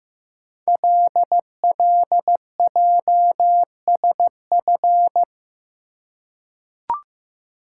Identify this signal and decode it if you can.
Morse code, heard as a keyed tone, about 15 wpm, 700 Hz. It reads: LLJSF
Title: Code Groups - 5 Letters - 15wpm